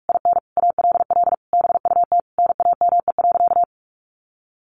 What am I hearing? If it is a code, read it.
Morse code, heard as a keyed tone, about 44 wpm, 700 Hz. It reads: IN ALL BUT NAME.
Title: Sentences from Top 1500 Words 45wpm